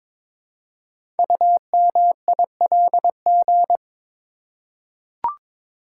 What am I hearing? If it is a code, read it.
Morse code, heard as a keyed tone, about 22 wpm, 700 Hz. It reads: UMILG
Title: Code Groups - 5 Letters - 22wpm